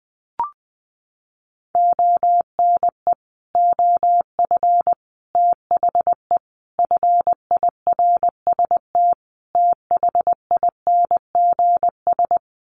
Morse code, heard as a keyed tone, about 19 wpm, 700 Hz. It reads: ONE OF THE FIRST THINGS
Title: Sets of 5 Words Encore 20wpm